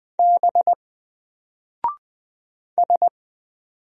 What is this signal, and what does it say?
Morse code, heard as a keyed tone, about 20 wpm, 700 Hz. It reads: B S
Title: Single Character (Letter:Number:Punctuation:Prosign) 20wpm